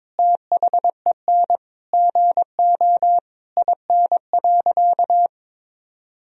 Morse code, heard as a keyed tone, about 22 wpm, 700 Hz. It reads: THEN GO IN.
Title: Sentences from Top 1000 Words - Review Part 4 - 22wpm